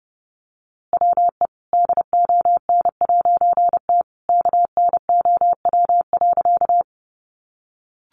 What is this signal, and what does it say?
Morse code, heard as a keyed tone, about 30 wpm, 700 Hz. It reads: WE DON'T KNOW.
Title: A Sound Of Thunder 30wpm